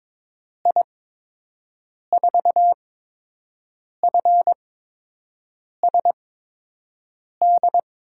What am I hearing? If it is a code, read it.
Morse code, heard as a keyed tone, about 22 wpm, spaced as for 6 wpm, 700 Hz. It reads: I4FSD
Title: USA Call Signs - 6 Characters - 8x Character Spacing - 22wpm